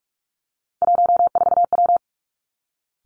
Morse code, heard as a keyed tone, about 46 wpm, 700 Hz. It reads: 14W
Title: Three Challenging Letter-Number Combinations 45wpm